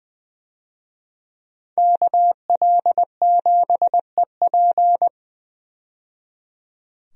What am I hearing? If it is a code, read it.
Morse code, heard as a keyed tone, about 20 wpm, 700 Hz. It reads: KL7EP
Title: DXCC-100 Call Signs - 5 Characters - 20wpm